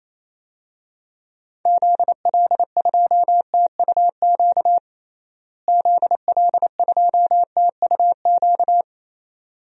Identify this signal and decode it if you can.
Morse code, heard as a keyed tone, about 28 wpm, 700 Hz. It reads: ZL2TUQ ZL2TUQ
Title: DXCC-100 Call Signs - 6 Characters - Repeated 2x - 28wpm